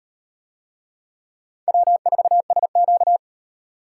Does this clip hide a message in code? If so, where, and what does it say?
Yes: Morse code from 1.7 to 3.2 s, heard as a keyed tone, about 38 wpm, 700 Hz. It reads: W4SQ